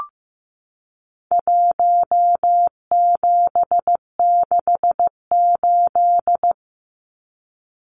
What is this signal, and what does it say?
Morse code, heard as a keyed tone, about 15 wpm, 700 Hz. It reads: 1768
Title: Numbers - Contest Style - 4-Digit 15wpm